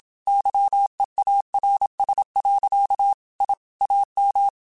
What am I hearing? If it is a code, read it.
Morse code, heard as a keyed tone, about 27 wpm, 800 Hz. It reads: YEARS. I AM
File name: QSOs / QSOs-mp3s / kc7sac@25